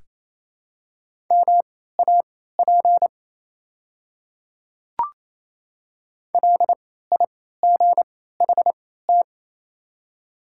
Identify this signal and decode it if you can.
Morse code, heard as a keyed tone, about 28 wpm, spaced as for 16 wpm, 700 Hz. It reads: MAP LIGHT
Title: 3x Character Spacing - Top 500 Words - 28wpm